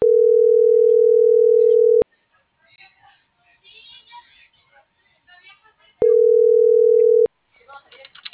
Ringback tone (North America), 440 + 480 Hz.